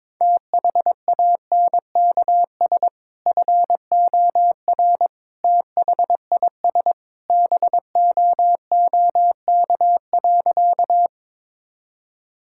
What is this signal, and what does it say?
Morse code, heard as a keyed tone, about 22 wpm, 700 Hz. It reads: THANKS FOR THIS BOOK.
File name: practice / 22wpm / Sentences with 2-Word Phrases - 22wpm